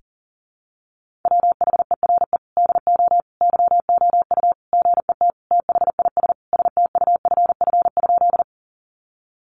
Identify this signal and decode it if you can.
Morse code, heard as a keyed tone, about 40 wpm, 700 Hz. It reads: WHERE DO YOU GET THIS STUFF?
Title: Sentences from Top 1400 Words 40wpm